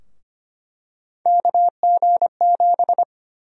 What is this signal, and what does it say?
Morse code, heard as a keyed tone, about 25 wpm, 700 Hz. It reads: KG7
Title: USA Call Sign Prefixes - 25wpm